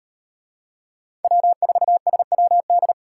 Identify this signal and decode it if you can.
Morse code, heard as a keyed tone, about 38 wpm, 700 Hz. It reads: W4SWD